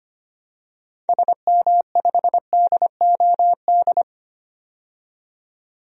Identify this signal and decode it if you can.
Morse code, heard as a keyed tone, about 25 wpm, 700 Hz. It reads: SM5DOD
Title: DXCC-100 Call Signs - 6 Characters - 25wpm